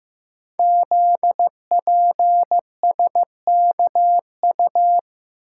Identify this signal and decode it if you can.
Morse code, heard as a keyed tone, about 15 wpm, 700 Hz. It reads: ZPSKU
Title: Code Groups - 5 Letter:Numbers - 15wpm